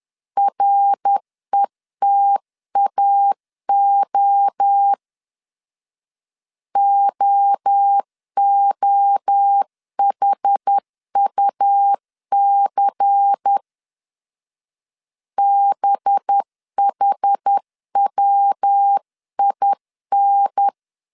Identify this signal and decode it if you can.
Morse code, heard as a keyed tone, about 10 wpm, 800 Hz. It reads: RETAO OOHUC BHWIN
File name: CW / DET / 1 6 1500C S C 10wpm 4252